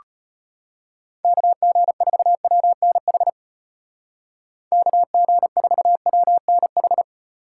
Morse code, heard as a keyed tone, about 33 wpm, 700 Hz. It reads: KG4WNH KG4WNH